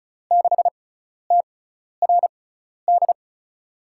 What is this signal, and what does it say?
Morse code, heard as a keyed tone, about 36 wpm, spaced as for 12 wpm, 700 Hz. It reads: 6TRD
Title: USA Call Signs - 6x Character Spacing - 35wpm